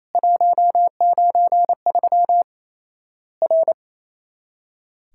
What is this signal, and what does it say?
Morse code, heard as a keyed tone, about 28 wpm, 700 Hz. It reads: 193 R
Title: Worked All Europe DX Contest - QTC Exchange - 28wpm